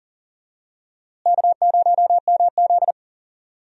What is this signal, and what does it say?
Morse code, heard as a keyed tone, about 38 wpm, 700 Hz. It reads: K0MZ
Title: USA Call Signs - 4 Characters - Repeated 3x - 40wpm